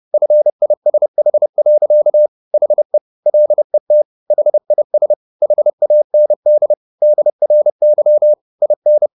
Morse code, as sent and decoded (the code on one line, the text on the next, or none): ..-. .. ... .... .-.-.- / .... / .-.. - / .... .. ... / .... .- -. -.. / -.. .-. -.-- / .. -.
FISH. HE LET HIS HAND DRY IN